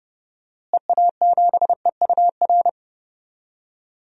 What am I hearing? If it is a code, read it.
Morse code, heard as a keyed tone, about 30 wpm, 700 Hz. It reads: EA7EUR
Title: DXCC-100 Call Signs - 6 Characters - Repeated 2x - 30wpm